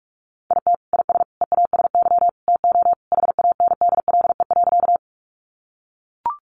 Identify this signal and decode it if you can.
Morse code, heard as a keyed tone, about 44 wpm, 700 Hz. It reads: IT IS EASY TO HANDLE.
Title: Sentences from Top 1100 Words 45wpm